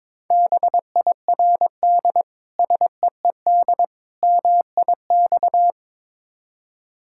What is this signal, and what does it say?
Morse code, heard as a keyed tone, about 23 wpm, 690 Hz. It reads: BIRD SEED MIX
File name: Difficult Words with Hints - 22wpm